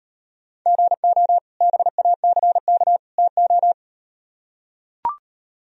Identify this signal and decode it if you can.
Morse code, heard as a keyed tone, about 35 wpm, 700 Hz. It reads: GO BACK TO